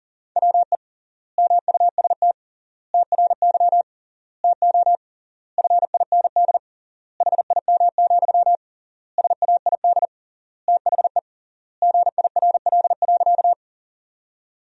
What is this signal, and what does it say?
Morse code, heard as a keyed tone, about 40 wpm, 700 Hz. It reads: WE MUST TRY TO FIND HIM, SAID THE GIRL.
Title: 3x Word-Spacing - Sentences from Top 400 Words - 40wpm